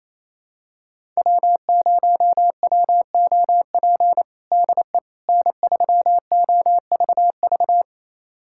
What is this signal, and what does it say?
Morse code, heard as a keyed tone, about 28 wpm, 700 Hz. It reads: W0WOP DE N3OVV
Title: USA Callsign DE Callsign - 5 Characters - 28wpm